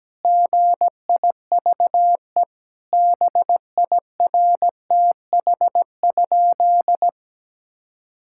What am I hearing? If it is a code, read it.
Morse code, heard as a keyed tone, about 17 wpm, 700 Hz. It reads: GIVE BIRTH?
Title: Sentences from Top 1000 Words - Review Part 5 - 17wpm